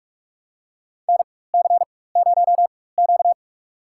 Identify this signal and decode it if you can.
Morse code, heard as a keyed tone, about 42 wpm, spaced as for 20 wpm, 700 Hz. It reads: NC0Q
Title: USA Call Signs - 4 Characters - 4x Character Spacing - 45wpm